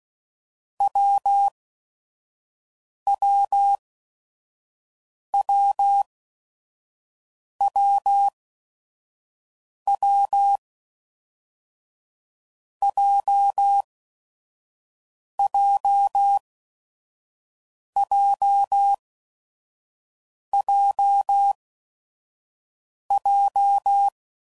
Morse code, heard as a keyed tone, about 16 wpm, spaced as for 5 wpm, 800 Hz. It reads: WWWWW JJJJJ
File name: Combinations / Difficult Combos / AWJ1